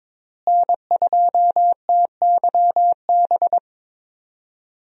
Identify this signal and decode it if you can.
Morse code, heard as a keyed tone, about 21 wpm, 700 Hz. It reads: N2TYB
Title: USA Call Signs - 5 Characters - Repeated 3x - 22wpm